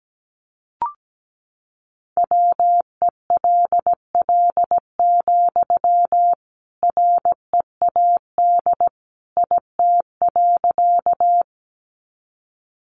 Morse code, heard as a keyed tone, about 17 wpm, 700 Hz. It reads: WELL, READ IT.
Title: Sentences from Top 200 Words 17wpm